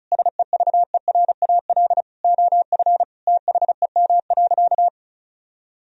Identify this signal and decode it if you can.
Morse code, heard as a keyed tone, about 36 wpm, 700 Hz. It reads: SEVERAL OF THEM.
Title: Sentences from Top 1000 Words - Review Part 2 - 35wpm